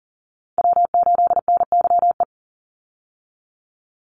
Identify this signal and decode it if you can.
Morse code, heard as a keyed tone, about 38 wpm, 700 Hz. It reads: W8NYE
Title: USA Call Signs 40wpm